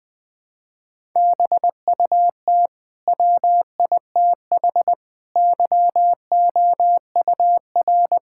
Morse code, heard as a keyed tone, about 20 wpm, 700 Hz. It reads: BUT WITH YOUR
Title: Sentences from Top 1000 Words - Review Part 6 - 20wpm